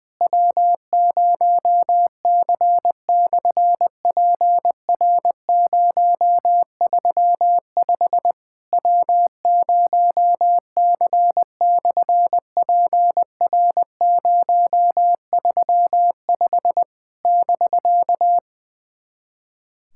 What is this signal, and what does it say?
Morse code, heard as a keyed tone, about 20 wpm, 700 Hz. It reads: W0C/PR035 W0C/PR035 <BK>
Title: SOTA QSO Practice 20wpm